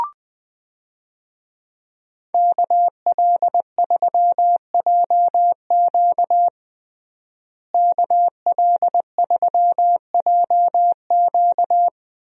Morse code, heard as a keyed tone, about 20 wpm, 700 Hz. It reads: KL3JQ KL3JQ
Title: USA Call Signs - 5 Characters - Repeated 2x - 20wpm